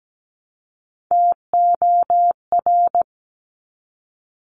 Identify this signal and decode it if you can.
Morse code, heard as a keyed tone, about 17 wpm, 700 Hz. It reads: TOR
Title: Top Three Letter Combinations in Words 17wpm